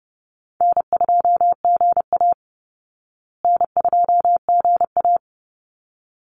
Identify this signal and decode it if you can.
Morse code, heard as a keyed tone, about 30 wpm, 700 Hz. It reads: N2GA N2GA
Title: Easy Call Signs Repeated 2x - 30wpm